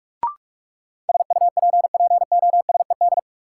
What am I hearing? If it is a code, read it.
Morse code, heard as a keyed tone, about 42 wpm, 700 Hz. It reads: SUPPOSED
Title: Sets of 3 Words - New Word Drill - 45wpm